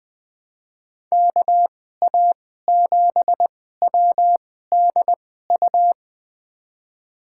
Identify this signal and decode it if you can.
Morse code, heard as a keyed tone, about 20 wpm, spaced as for 14 wpm, 700 Hz. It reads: KA7WDU
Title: USA Call Signs - 6 Characters - 2x Character Spacing - 20wpm